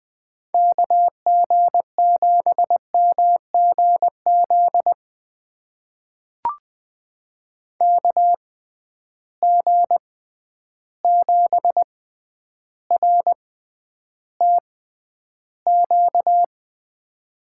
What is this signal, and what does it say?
Morse code, heard as a keyed tone, about 20 wpm, spaced as for 10 wpm, 700 Hz. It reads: KG7MGZ KG7RTQ
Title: USA Call Signs - 6x Character Spacing - 20wpm